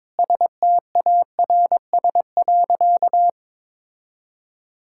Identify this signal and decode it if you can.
Morse code, heard as a keyed tone, about 22 wpm, 700 Hz. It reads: STARS.
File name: practice / 22wpm / Sentences with 4-Word Phrases - 22wpm